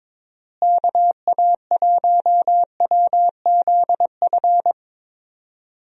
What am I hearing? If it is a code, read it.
Morse code, heard as a keyed tone, about 22 wpm, 700 Hz. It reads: KA1WZF
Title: USA Call Signs - 6 Characters - Repeated 3x - 22wpm